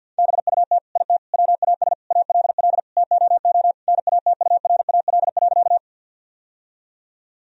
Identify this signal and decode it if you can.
Morse code, heard as a keyed tone, about 50 wpm, 700 Hz. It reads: BUT IT WAS ALL TOO NATURAL.
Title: Sentences from Top 1000 Words - Review Part 4 - 50wpm